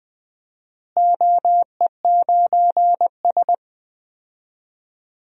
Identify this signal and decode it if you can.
Morse code, heard as a keyed tone, about 20 wpm, 700 Hz. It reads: OE9S
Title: DXCC-100 Call Signs - 4 Characters - Repeated 2x - 20wpm